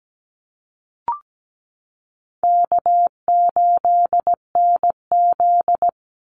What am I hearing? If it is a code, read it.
Morse code, heard as a keyed tone, about 17 wpm, 700 Hz. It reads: K8NZ
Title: Call Signs Repeated 3x - 17wpm